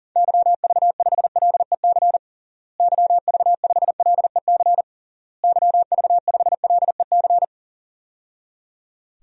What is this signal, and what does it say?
Morse code, heard as a keyed tone, about 40 wpm, 700 Hz. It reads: YV5LEC YV5LEC YV5LEC
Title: DXCC-100 Call Signs - 6 Characters - Repeated 3x - 40wpm